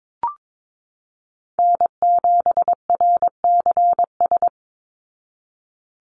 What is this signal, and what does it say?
Morse code, heard as a keyed tone, about 22 wpm, 700 Hz. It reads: N7RCS
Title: Call Signs Repeated 3x - 22wpm